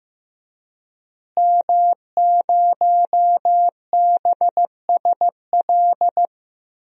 Morse code, heard as a keyed tone, about 15 wpm, 700 Hz. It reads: M0BSL
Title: DXCC-100 Call Signs - 5 Characters - Repeated 2x - 15wpm